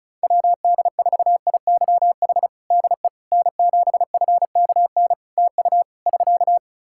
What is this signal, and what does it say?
Morse code, heard as a keyed tone, about 36 wpm, 700 Hz. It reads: WD4IYH DE N7FKN TU <SK>
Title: Practice QSO - Exchange 4 - Ending - 35wpm